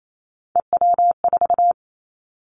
Morse code, heard as a keyed tone, about 28 wpm, 700 Hz. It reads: EW4
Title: Callsign Prefixes for Top 50 DXCC - 28wpm